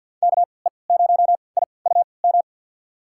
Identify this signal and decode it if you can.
Morse code, heard as a keyed tone, about 46 wpm, spaced as for 27 wpm, 700 Hz. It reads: KE0IUM
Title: USA Call Signs - 3x Character Spacing - 50wpm